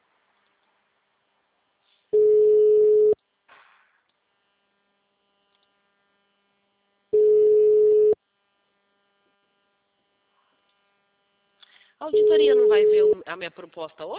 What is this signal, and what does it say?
Ringback tone (Europe, 425 Hz), with speech.
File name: out- -2017- - - .66730